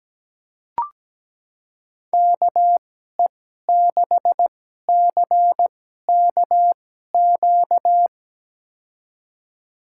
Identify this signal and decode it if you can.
Morse code, heard as a keyed tone, about 17 wpm, spaced as for 12 wpm, 700 Hz. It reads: KE6CKQ
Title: USA Call Signs - 6 Characters - 2x Character Spacing - 17wpm